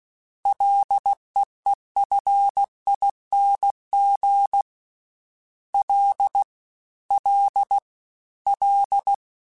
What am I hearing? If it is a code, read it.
Morse code, heard as a keyed tone, about 16 wpm, 800 Hz. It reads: LEEFING L L L
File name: Ls / Single L m3s / Single L Prefixe 3x2 Twice @15